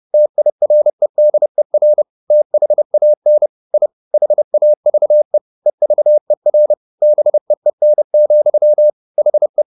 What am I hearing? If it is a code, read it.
Morse code, heard as a keyed tone, about 30 wpm, 600 Hz. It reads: TIREDER THAN I HAVE EVER BEEN, HE